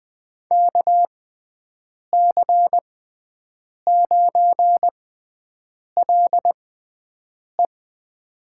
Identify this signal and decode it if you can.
Morse code, heard as a keyed tone, about 20 wpm, spaced as for 7 wpm, 700 Hz. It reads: KC9LE
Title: USA Call Signs - 5 Characters - 6x Character Spacing - 20wpm